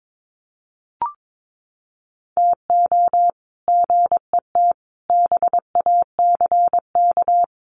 Morse code, heard as a keyed tone, about 22 wpm, 700 Hz. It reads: TO GET BACK
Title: Sets of 5 Words Encore 22wpm